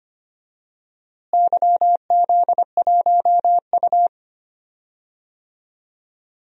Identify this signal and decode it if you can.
Morse code, heard as a keyed tone, about 25 wpm, 700 Hz. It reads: YZ1U
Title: DXCC-100 Call Signs - 4 Characters - Repeated 3x - 25wpm